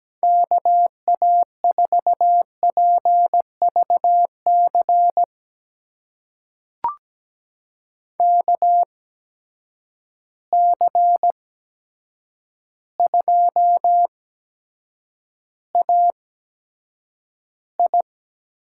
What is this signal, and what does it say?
Morse code, heard as a keyed tone, about 17 wpm, spaced as for 8 wpm, 700 Hz. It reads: KA4PVC KC2AI
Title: USA Call Signs - 8x Character Spacing - 17wpm